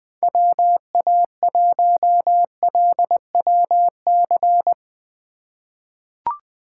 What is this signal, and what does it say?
Morse code, heard as a keyed tone, about 20 wpm, 700 Hz. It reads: WA1LWC
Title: K1USN SST QSO Practice - 20wpm